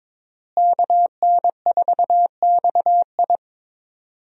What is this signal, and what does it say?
Morse code, heard as a keyed tone, about 21 wpm, 700 Hz. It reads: KN4XI
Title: USA Call Signs - 5 Characters - Repeated 3x - 22wpm